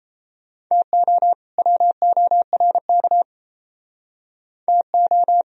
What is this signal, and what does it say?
Morse code, heard as a keyed tone, about 37 wpm, 700 Hz. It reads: TO WORK TO